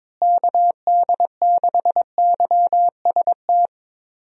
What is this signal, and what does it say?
Morse code, heard as a keyed tone, about 22 wpm, 700 Hz. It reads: KD6YST
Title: USA Call Signs - 6 Characters - Repeated 3x - 22wpm